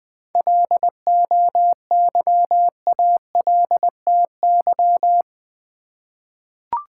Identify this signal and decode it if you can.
Morse code, heard as a keyed tone, about 20 wpm, 700 Hz. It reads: LOYALTY
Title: 7 Letter Words - First 3 Letters Given - 20wpm